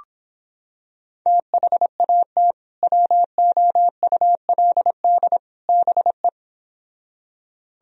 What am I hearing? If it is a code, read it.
Morse code, heard as a keyed tone, about 26 wpm, 700 Hz. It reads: THAT WOULD BE